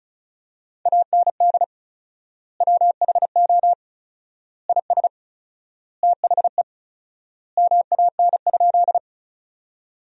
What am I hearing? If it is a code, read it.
Morse code, heard as a keyed tone, about 34 wpm, 700 Hz. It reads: AND WHO IS THE MAN?
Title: 4x Word-Spacing - Sentences from Top 200 Words - 35wpm